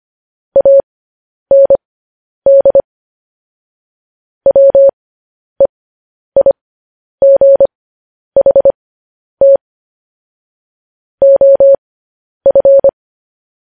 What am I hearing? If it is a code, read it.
Morse code, heard as a keyed tone, about 25 wpm, spaced as for 10 wpm, 550 Hz. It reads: AND WEIGHT OF